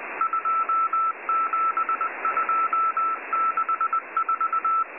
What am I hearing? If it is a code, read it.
Morse code, heard as a keyed tone, about 20 wpm, 1330 Hz, with noise. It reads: 27264